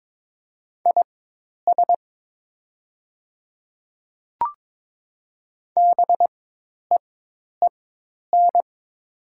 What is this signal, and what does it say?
Morse code, heard as a keyed tone, about 22 wpm, spaced as for 10 wpm, 700 Hz. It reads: IS BEEN
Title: 4x Character Spacing - Top 100 Words - 22wpm